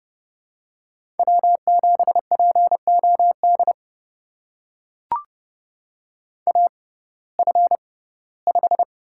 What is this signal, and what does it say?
Morse code, heard as a keyed tone, about 30 wpm, spaced as for 18 wpm, 700 Hz. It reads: W7POD AF5
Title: USA Call Signs - 5 Characters - 6x Character Spacing - 30wpm